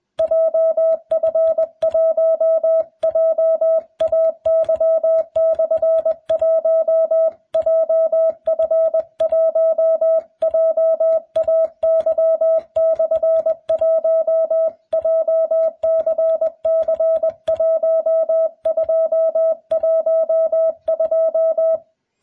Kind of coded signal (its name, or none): Morse code